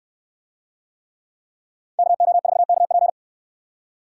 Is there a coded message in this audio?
Yes: Morse code from 2.0 to 3.1 s, heard as a keyed tone, about 64 wpm, 700 Hz. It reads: KO4KP